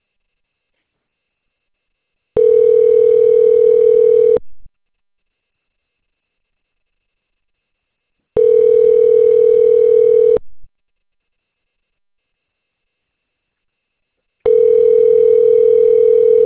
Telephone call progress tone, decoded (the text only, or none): ringback tone